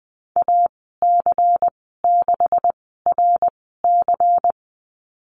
Morse code, heard as a keyed tone, about 19 wpm, spaced as for 14 wpm, 700 Hz. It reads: AC6RC
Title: 128 Lesson 21g - Number 6 - Call Signs 20wpm